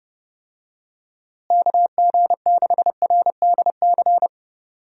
Morse code, heard as a keyed tone, about 30 wpm, 700 Hz. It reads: KG6RDC
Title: USA Call Signs - 6 Characters - Repeated 2x - 30wpm